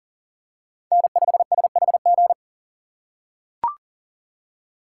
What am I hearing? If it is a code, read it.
Morse code, heard as a keyed tone, about 40 wpm, 700 Hz. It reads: N5SHG
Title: USA Call Signs - 3x Character Spacing - 40wpm